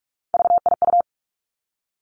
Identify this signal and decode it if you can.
Morse code, heard as a keyed tone, about 46 wpm, 700 Hz. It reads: VIU
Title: Three Challenging Letter-Number Combinations 45wpm